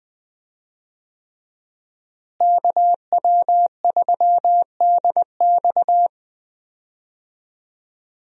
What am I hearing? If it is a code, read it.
Morse code, heard as a keyed tone, about 20 wpm, 700 Hz. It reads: KW3DX
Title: USA Call Signs - 5 Characters - Repeated 3x - 20wpm